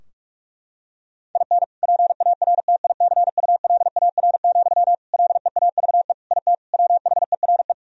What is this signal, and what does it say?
Morse code, heard as a keyed tone, about 46 wpm, 700 Hz. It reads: IN PARTICULAR, LEAVE IT WHERE
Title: Sentences from Top 1000 Words - Review Part 4 - 45wpm